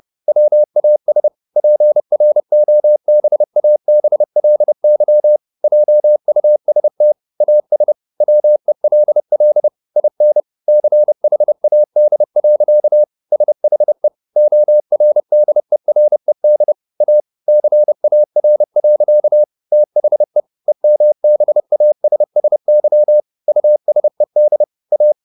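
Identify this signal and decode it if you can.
Morse code, heard as a keyed tone, about 30 wpm, 600 Hz. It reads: WAS PROBABLY JUST AS WELL IN CHAD. SHE ORDERED A CAR. THE EMBASSY USED A
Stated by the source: Morse code at 30 wpm, the novel Never by Ken Follett in English, from CW med Gnister.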